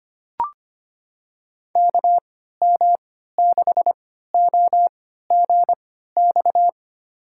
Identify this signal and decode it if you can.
Morse code, heard as a keyed tone, about 25 wpm, spaced as for 14 wpm, 700 Hz. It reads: KM6OGX
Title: USA Call Signs - 6 Characters - 3x Character Spacing - 25wpm